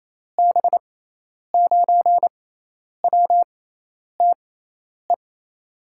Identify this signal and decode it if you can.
Morse code, heard as a keyed tone, about 28 wpm, spaced as for 10 wpm, 700 Hz. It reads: B9WTE
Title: USA Call Signs - 6 Characters - 6x Character Spacing - 28wpm